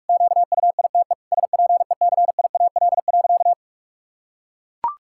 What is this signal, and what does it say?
Morse code, heard as a keyed tone, about 41 wpm, 700 Hz. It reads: QUITE SPECIAL.
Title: Sentences from Top 1600 Words 45wpm